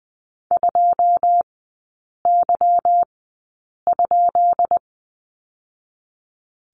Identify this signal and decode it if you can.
Morse code, heard as a keyed tone, about 20 wpm, spaced as for 8 wpm, 700 Hz. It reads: 2Y?
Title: 191 Lesson 31e - Letter v - Three Characters 20wpm